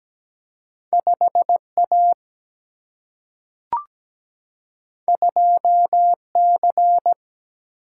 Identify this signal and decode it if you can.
Morse code, heard as a keyed tone, about 17 wpm, 700 Hz. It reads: 5A 2C
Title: Single-Letter Single-Number 17wpm